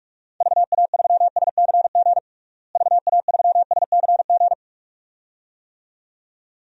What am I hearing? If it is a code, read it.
Morse code, heard as a keyed tone, about 46 wpm, 700 Hz. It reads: VA3SCG VA3SCG
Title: DXCC-100 Call Signs - 6 Characters - Repeated 2x - 45wpm